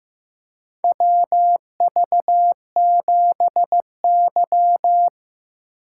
Morse code, heard as a keyed tone, about 15 wpm, 700 Hz. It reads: WV7Y
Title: USA Call Signs - 4 Characters - 2x Character Spacing - 15wpm